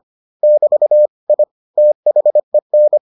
Morse code, heard as a keyed tone, about 25 wpm, 600 Hz. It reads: <BT> I THEN